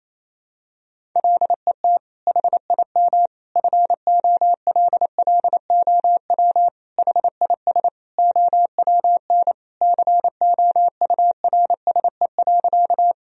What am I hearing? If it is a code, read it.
Morse code, heard as a keyed tone, about 27 wpm, 700 Hz. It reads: LET HIM FOLLOW HIS OWN COURSE.
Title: Sentences from Top 1000 Words - Review Part 1 - 28wpm